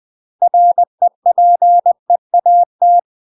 Morse code, heard as a keyed tone, about 20 wpm, 700 Hz. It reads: REPEAT
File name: UK MASTER QSOs TRAINING / 20WPM / UK MASTER QSOs 0034